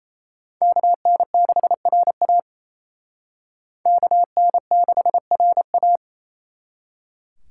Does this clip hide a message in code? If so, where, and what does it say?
Yes: Morse code from 0.6 to 6.0 s, heard as a keyed tone, about 37 wpm, 700 Hz. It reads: KN6RA KN6RA